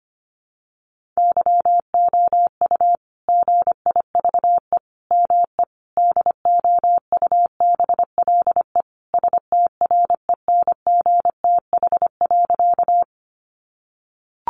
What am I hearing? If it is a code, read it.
Morse code, heard as a keyed tone, about 25 wpm, 700 Hz. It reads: YOU GIVE ME DOUBLE STRENGTH.
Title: Sentences from Top 1400 Words 25wpm